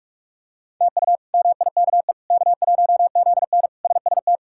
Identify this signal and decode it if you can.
Morse code, heard as a keyed tone, about 41 wpm, 700 Hz. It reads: TU MIKE K1ZN SST
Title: K1USN SST QSO Practice - 45wpm